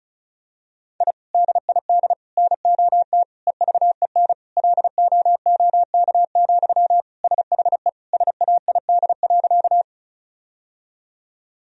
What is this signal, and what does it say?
Morse code, heard as a keyed tone, about 36 wpm, 700 Hz. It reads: I DID NOT EVEN LOOK, SHE SAID.
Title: Sentences from Top 1000 Words - Review Part 1 - 35wpm